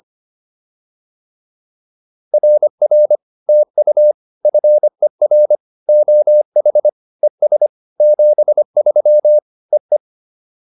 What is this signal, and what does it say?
Morse code, heard as a keyed tone, about 25 wpm, 590 Hz. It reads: RR TU FER OH ES 73 EE
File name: POTA209a 25